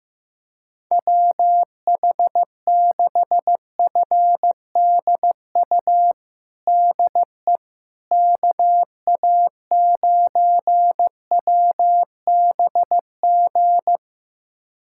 Morse code, heard as a keyed tone, about 15 wpm, 700 Hz. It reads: WH6FDU DE KA9WBG
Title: USA Callsign DE Callsign - 6 Characters - 15wpm